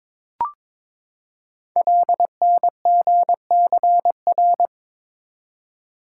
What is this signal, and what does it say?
Morse code, heard as a keyed tone, about 22 wpm, 700 Hz. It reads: LNGCR
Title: Code Groups - 5 Letters - 22wpm